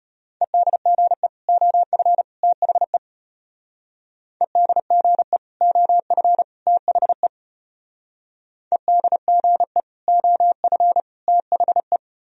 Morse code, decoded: EDGE OF THE EDGE OF THE EDGE OF THE

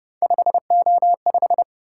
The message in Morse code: 5O5

